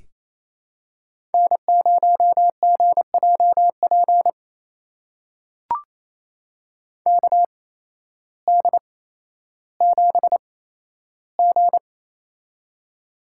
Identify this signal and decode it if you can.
Morse code, heard as a keyed tone, about 28 wpm, spaced as for 13 wpm, 700 Hz. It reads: N0GJP KD7G
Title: USA Call Signs - 8x Character Spacing - 28wpm